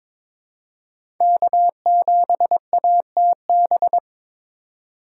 Morse code, heard as a keyed tone, about 22 wpm, 700 Hz. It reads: K7ATB
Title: USA Call Signs - 4x Character Spacing - 22wpm